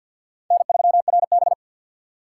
Morse code, heard as a keyed tone, about 48 wpm, 700 Hz. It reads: N3RB
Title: USA Call Signs - 2x Character Spacing - 50wpm